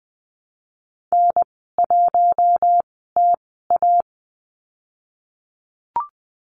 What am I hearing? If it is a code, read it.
Morse code, heard as a keyed tone, about 20 wpm, spaced as for 14 wpm, 700 Hz. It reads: N1TA
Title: 088 Lesson 15g - Number 5 - Call Signs 20wpm